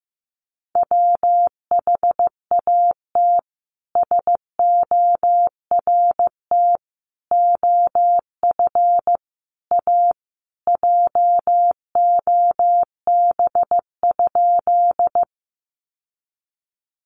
Morse code, heard as a keyed tone, about 15 wpm, 700 Hz. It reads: WHAT SORT OF A JOB?
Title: Sentences from Top 1300 Words 15wpm